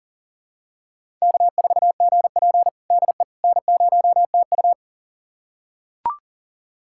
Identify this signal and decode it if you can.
Morse code, heard as a keyed tone, about 40 wpm, 700 Hz. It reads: K4GP DE N0TU
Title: USA Callsign DE Callsign - 4 Characters - 40wpm